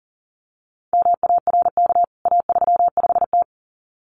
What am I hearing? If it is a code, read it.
Morse code, heard as a keyed tone, about 40 wpm, 700 Hz. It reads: MARK A35T
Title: CWT Exchange 40wpm